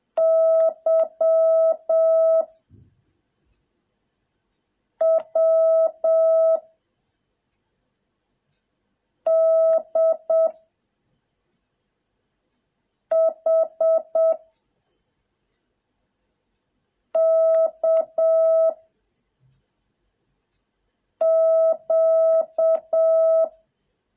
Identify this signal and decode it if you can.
Morse code, heard as a keyed tone, about 7 wpm, 650 Hz. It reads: Y W D H K Q